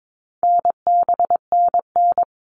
Morse code, heard as a keyed tone, about 22 wpm, 700 Hz. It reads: NBNN
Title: All Cut Numbers - 4-Digit 22wpm